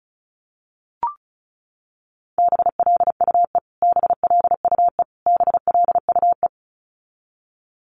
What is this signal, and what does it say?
Morse code, heard as a keyed tone, about 34 wpm, 700 Hz. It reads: BLUE BLUE BLUE
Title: Top 500 Words Repeated 35wpm